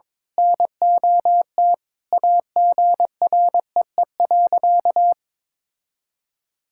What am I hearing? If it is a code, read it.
Morse code, heard as a keyed tone, about 22 wpm, 700 Hz. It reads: NOT AGREE.
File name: practice / 22wpm / Sentences with 4-Word Phrases - 22wpm